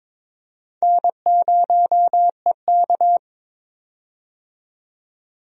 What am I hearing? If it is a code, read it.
Morse code, heard as a keyed tone, about 22 wpm, 700 Hz. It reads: N0EK
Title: USA Call Signs - 4 Characters - 22wpm